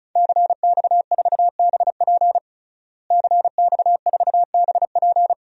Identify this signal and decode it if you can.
Morse code, heard as a keyed tone, about 35 wpm, 700 Hz. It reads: CX4BP CX4BP
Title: DXCC-100 Call Signs - 5 Characters - Repeated 3x - 35wpm